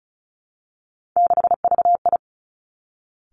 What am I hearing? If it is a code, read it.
Morse code, heard as a keyed tone, about 36 wpm, 700 Hz. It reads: 6VI
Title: Three Challenging Letter-Number Combinations 35wpm